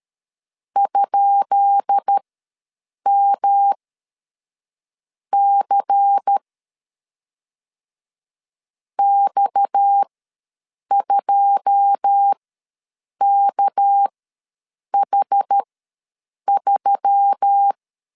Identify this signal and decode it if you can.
Morse code, heard as a keyed tone, about 13 wpm, spaced as for 6 wpm, 800 Hz. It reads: ?MC X2KH3
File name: LEZIONI / 10 SIGNS / 10B INT ws 12wpm 3527